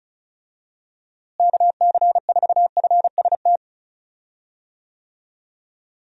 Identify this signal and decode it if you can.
Morse code, heard as a keyed tone, about 34 wpm, 700 Hz. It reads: KC4FST
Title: USA Call Signs - 6 Characters - Repeated 2x - 35wpm